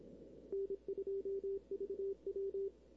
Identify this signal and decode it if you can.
Morse code, heard as a keyed tone, about 26 wpm, 410 Hz. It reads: N2VW
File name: arrl cw dx 2005